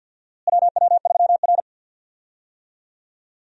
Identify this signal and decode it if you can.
Morse code, heard as a keyed tone, about 46 wpm, 700 Hz. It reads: WW3R
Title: USA Call Signs - 4 Characters - Repeated 2x - 50wpm